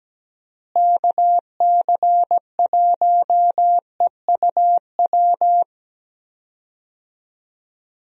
Morse code, heard as a keyed tone, about 17 wpm, 700 Hz. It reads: KC1EUW
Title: USA Call Signs - 6 Characters - Repeated 3x - 17wpm